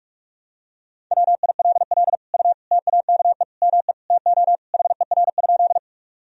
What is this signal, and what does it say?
Morse code, heard as a keyed tone, about 46 wpm, 700 Hz. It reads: WILL U TAKE ME TO HER?
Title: Sentences from Top 200 Words and Abbreviations 45wpm